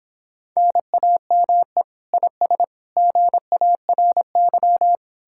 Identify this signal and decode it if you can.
Morse code, heard as a keyed tone, about 29 wpm, 700 Hz. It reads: NAME IS GARY